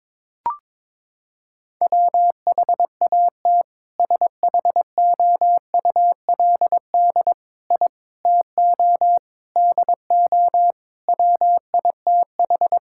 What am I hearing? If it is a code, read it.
Morse code, heard as a keyed tone, about 22 wpm, 700 Hz. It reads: WHAT SHOULD I TO DO WITH
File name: practice / 22wpm / Sentences with 4-Word Phrases - 22wpm